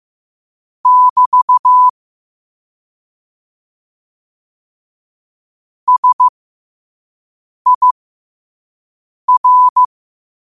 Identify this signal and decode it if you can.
Morse code, heard as a keyed tone, about 15 wpm, spaced as for 5 wpm, 1000 Hz. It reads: <BT> SIR